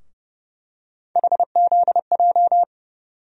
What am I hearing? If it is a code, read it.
Morse code, heard as a keyed tone, about 30 wpm, 700 Hz. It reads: HZJ
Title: USA Call Sign Postfixes - 3 Letters 30wpm